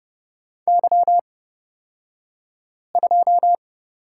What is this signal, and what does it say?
Morse code, heard as a keyed tone, about 30 wpm, spaced as for 5 wpm, 700 Hz. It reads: Y2
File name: Lesson 38d - Letter q - Two Characters 30wpm